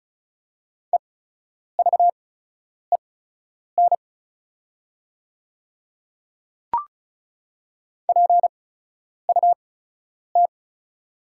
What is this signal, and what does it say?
Morse code, heard as a keyed tone, about 36 wpm, spaced as for 10 wpm, 690 Hz. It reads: EVEN PUT
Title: 8x Character Spacing - Top 300 Words - 35wpm